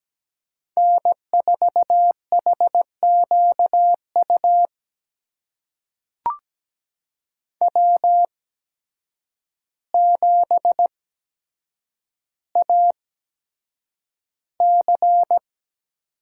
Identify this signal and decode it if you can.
Morse code, heard as a keyed tone, about 17 wpm, spaced as for 8 wpm, 700 Hz. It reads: N4HQU W7AC
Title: USA Call Signs - 5 Characters - 8x Character Spacing - 17wpm